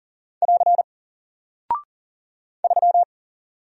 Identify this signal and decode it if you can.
Morse code, heard as a keyed tone, about 40 wpm, 700 Hz. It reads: <AR> 3
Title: Single Character (Letter:Number:Punctuation:Prosign) Alt 40wpm